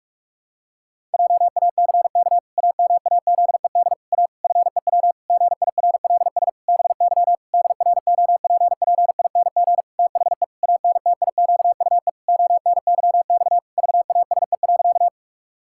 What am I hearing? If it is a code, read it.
Morse code, heard as a keyed tone, about 46 wpm, 700 Hz. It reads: JACK AMAZED A FEW GIRLS BY DROPPING THE ANTIQUE ONYX VASE.
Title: Pangram Challenge v2 - 45wpm